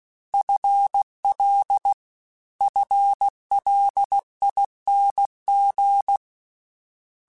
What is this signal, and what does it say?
Morse code, heard as a keyed tone, about 16 wpm, 800 Hz. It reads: FL FLING
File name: Combinations / Combinations-mp3s / Combinations@15